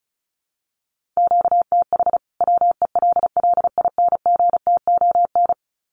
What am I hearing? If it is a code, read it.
Morse code, heard as a keyed tone, about 34 wpm, 700 Hz. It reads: QTH WELLINGTON
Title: Common QSO Phrases 35wpm